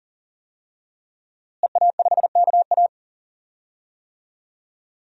Morse code, heard as a keyed tone, about 40 wpm, 700 Hz. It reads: EA5KA
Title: DXCC-100 Call Signs - 5 Characters - Repeated 3x - 40wpm